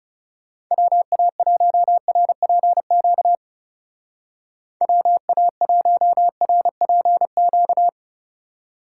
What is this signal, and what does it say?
Morse code, heard as a keyed tone, about 36 wpm, 700 Hz. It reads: WA1RPQ WA1RPQ
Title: Speed Racing - USA 2x3 Call Signs 30wpm